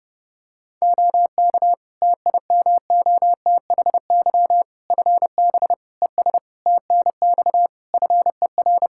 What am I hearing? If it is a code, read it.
Morse code, heard as a keyed tone, about 32 wpm, 700 Hz. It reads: OK TIMOTHY FB ES TNX FER
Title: Practice QSO - Exchange 2 - Info - 30wpm